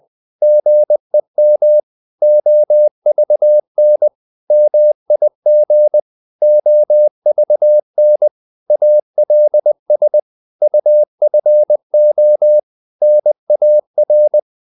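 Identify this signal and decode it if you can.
Morse code, heard as a keyed tone, about 20 wpm, 600 Hz. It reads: GEM OVN MIG OVN ALS UFO NAR
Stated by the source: Morse code practice of short Danish words at 20 wpm, from CW med Gnister.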